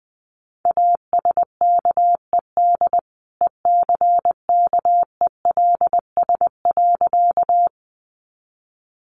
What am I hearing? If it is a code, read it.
Morse code, heard as a keyed tone, about 19 wpm, 700 Hz. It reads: ASKED ECKELS.
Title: A Sound Of Thunder 20wpm